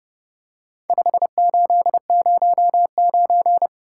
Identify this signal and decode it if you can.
Morse code, heard as a keyed tone, about 30 wpm, 700 Hz. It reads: 5809